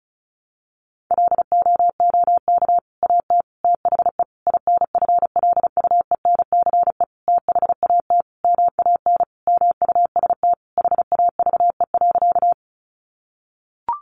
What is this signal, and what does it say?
Morse code, heard as a keyed tone, about 34 wpm, 700 Hz. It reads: LOOK AT THE INFLUENCE THAT MAN MUST HAVE.
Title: Sentences from Top 1100 Words 35wpm